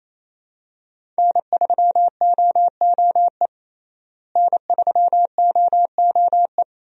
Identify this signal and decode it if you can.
Morse code, heard as a keyed tone, about 28 wpm, 700 Hz. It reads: N3OOE N3OOE
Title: USA Call Signs - 5 Characters - Repeated 2x - 28wpm